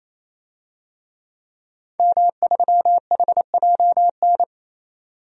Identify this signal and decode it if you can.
Morse code, heard as a keyed tone, about 28 wpm, 700 Hz. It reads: M3HJN
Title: DXCC-100 Call Signs - 5 Characters - 28wpm